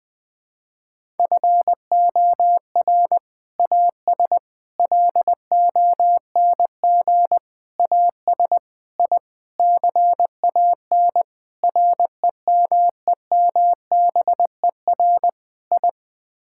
Morse code, heard as a keyed tone, about 20 wpm, 700 Hz. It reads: FOR AS LONG AS I CAN REMEMBER I
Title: Sets of 8 Words - 20wpm